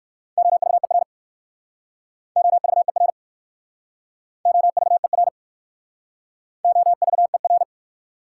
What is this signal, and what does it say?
Morse code, heard as a keyed tone, about 55 wpm, 700 Hz. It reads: OVER OVER OVER OVER
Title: Speed-Racing Prefixes and Suffixes 45wpm